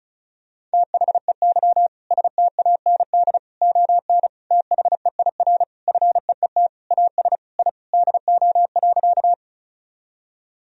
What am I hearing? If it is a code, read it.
Morse code, heard as a keyed tone, about 36 wpm, 700 Hz. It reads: THEY STAND ON THEIR FEET AS I DO.
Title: Sentences from Top 1000 Words - Review Part 1 - 35wpm